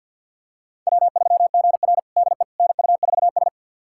Morse code, heard as a keyed tone, about 53 wpm, 700 Hz. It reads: W3GR DE NU4S
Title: USA Callsign DE Callsign - 4 Characters - 50wpm